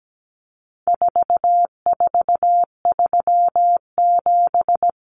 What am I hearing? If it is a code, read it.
Morse code, heard as a keyed tone, about 17 wpm, 700 Hz. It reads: 4437
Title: Numbers - 4-Digit 17wpm